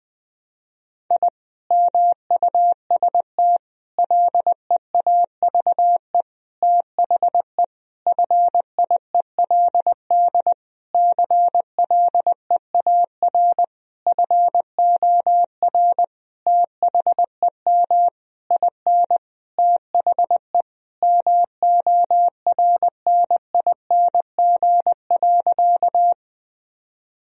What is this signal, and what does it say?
Morse code, heard as a keyed tone, about 20 wpm, 700 Hz. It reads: I MUST LEAVE THE FIELD CLEAR FOR THEM IN THE MORNING.
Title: Sentences from Top 1000 Words - Review Part 2 - 20wpm